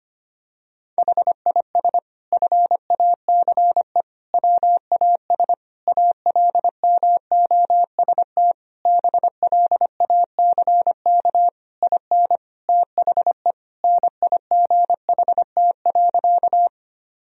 Morse code, heard as a keyed tone, about 25 wpm, 700 Hz. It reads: HIS FACE WAS ALMOST BLACK IN THE NIGHT.
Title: Sentences from Top 1000 Words - Review Part 4 - 25wpm